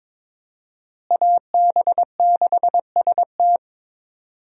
Morse code, heard as a keyed tone, about 22 wpm, 700 Hz. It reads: AB6ST
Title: USA Call Signs - 5 Characters - Repeated 3x - 22wpm